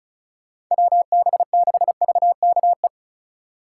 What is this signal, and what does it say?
Morse code, heard as a keyed tone, about 43 wpm, 700 Hz. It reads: WB6VKE
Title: Speed Racing - USA 2x3 Call Signs 35wpm